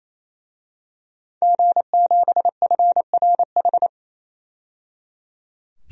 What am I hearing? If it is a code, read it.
Morse code, heard as a keyed tone, about 28 wpm, 700 Hz. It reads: G7FRH
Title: DXCC-100 Call Signs - 5 Characters - 28wpm